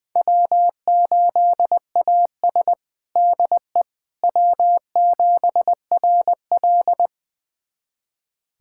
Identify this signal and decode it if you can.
Morse code, heard as a keyed tone, about 21 wpm, 700 Hz. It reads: W8AS DE W7RL
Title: USA Callsign DE Callsign - 4 Characters - 20wpm